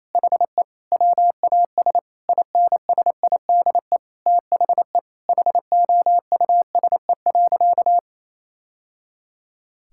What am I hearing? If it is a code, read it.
Morse code, heard as a keyed tone, about 27 wpm, 700 Hz. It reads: HE WAS INSIDE THE HOUSE.
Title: Sentences from Top 1000 Words - Review Part 6 - 28wpm